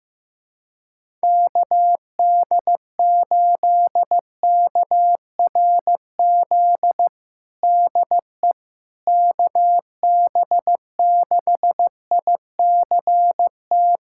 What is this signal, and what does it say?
Morse code, heard as a keyed tone, about 15 wpm, 700 Hz. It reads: KD8KRZ DE KB6ICT
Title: USA Callsign DE Callsign 15wpm